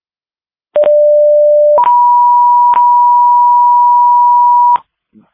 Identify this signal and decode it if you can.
Two-tone page, heard as a one-second tone, then a three-second tone, about 590 then 970 Hz, with noise. It is heard from 0.8 to 4.8 s.